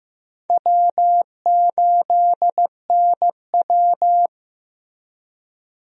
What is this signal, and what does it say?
Morse code, heard as a keyed tone, about 15 wpm, 700 Hz. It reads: W8NW
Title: USA Call Signs - 4 Characters - Repeated 3x - 15wpm